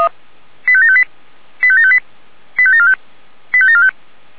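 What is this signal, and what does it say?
Five-tone selective call (ZVEI), heard as quick runs of five tones, tones of about 70 ms. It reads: 85758 85758 85745 85745
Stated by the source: ZVEI five-tone selective call 85827, received by SDR.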